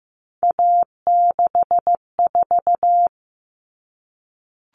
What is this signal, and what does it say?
Morse code, heard as a keyed tone, about 15 wpm, 700 Hz. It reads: A64
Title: Common Cut Numbers - 3-Digit 15wpm